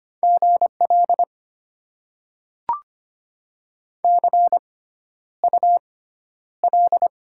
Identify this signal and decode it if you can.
Morse code, heard as a keyed tone, about 25 wpm, spaced as for 11 wpm, 700 Hz. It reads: GL CUL
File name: Top 90 QSO Elements - 6x Character Spacing - 25wpm